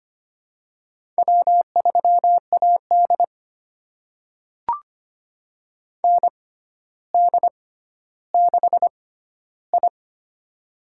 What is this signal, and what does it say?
Morse code, heard as a keyed tone, about 25 wpm, spaced as for 13 wpm, 700 Hz. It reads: W3AD ND6I
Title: USA Call Signs - 4 Characters - 6x Character Spacing - 25wpm